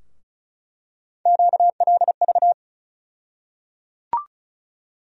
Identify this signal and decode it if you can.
Morse code, heard as a keyed tone, about 36 wpm, 700 Hz. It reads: QLV
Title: USA Call Sign Postfixes - 3 Letters 35wpm